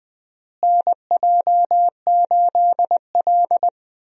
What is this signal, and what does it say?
Morse code, heard as a keyed tone, about 20 wpm, 700 Hz. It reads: NJ8L
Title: USA Call Signs - 4 Characters - Repeated 3x - 20wpm